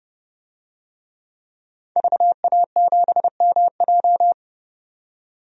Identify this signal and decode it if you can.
Morse code, heard as a keyed tone, about 32 wpm, 700 Hz. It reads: VA7MJ
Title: DXCC-100 Call Signs - 5 Characters - 30wpm